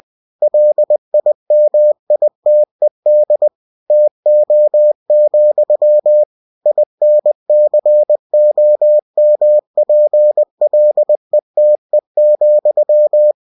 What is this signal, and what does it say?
Morse code, heard as a keyed tone, about 20 wpm, 600 Hz. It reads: LIMITED TO, INCOMPLETE,